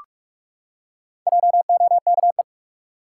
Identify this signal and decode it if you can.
Morse code, heard as a keyed tone, about 46 wpm, 700 Hz. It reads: JOKE
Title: Four Letter Words - 45wpm